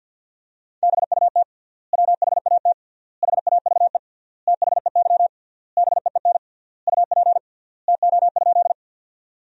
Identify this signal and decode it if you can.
Morse code, heard as a keyed tone, about 50 wpm, 700 Hz. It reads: BUT WHAT HAVE THEY BEEN UP TO?
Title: 3x Word-Spacing - Sentences from Top 100 Words - 50wpm